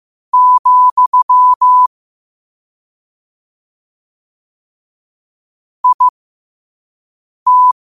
Morse code, heard as a keyed tone, about 15 wpm, spaced as for 5 wpm, 1000 Hz. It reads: , IT